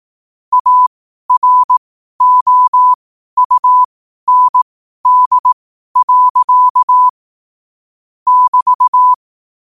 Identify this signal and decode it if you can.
Morse code, heard as a keyed tone, about 18 wpm, spaced as for 12 wpm, 1000 Hz. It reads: AROUND. <BT>
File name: qotdfiles / 12wpm